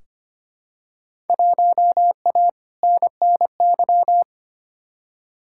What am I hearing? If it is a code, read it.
Morse code, heard as a keyed tone, about 25 wpm, 700 Hz. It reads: 1A NNY
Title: Field Day Exchanges - 25wpm